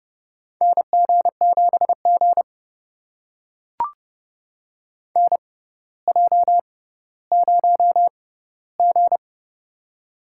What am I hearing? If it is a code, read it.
Morse code, heard as a keyed tone, about 31 wpm, spaced as for 15 wpm, 700 Hz. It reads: NG7G NJ0G
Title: USA Call Signs - 4 Characters - 6x Character Spacing - 30wpm